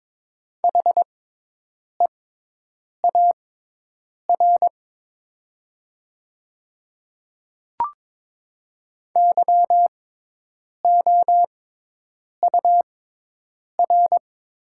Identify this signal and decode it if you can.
Morse code, heard as a keyed tone, about 22 wpm, spaced as for 8 wpm, 700 Hz. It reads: HEAR YOUR
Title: 6x Character Spacing - Top 300 Words - 22wpm